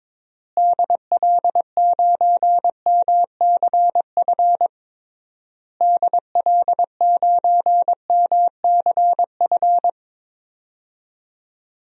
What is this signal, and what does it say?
Morse code, heard as a keyed tone, about 22 wpm, 700 Hz. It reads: DL9MCF DL9MCF
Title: DXCC-100 Call Signs - 6 Characters - Repeated 3x - 22wpm